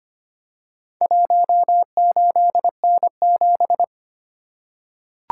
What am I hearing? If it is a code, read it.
Morse code, heard as a keyed tone, about 25 wpm, 700 Hz. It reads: 18N7
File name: practice / 25wpm / Four Letter-Number Combinations - 25wpm